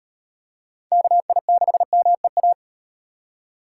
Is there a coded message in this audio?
Yes: Morse code from 0.9 to 2.5 s, heard as a keyed tone, about 38 wpm, 700 Hz. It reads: KI6MEA